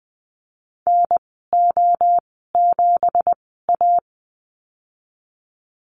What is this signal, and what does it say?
Morse code, heard as a keyed tone, about 20 wpm, spaced as for 15 wpm, 700 Hz. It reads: NO7A
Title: 213 Lesson 34g - Number 8 - Call Signs 20wpm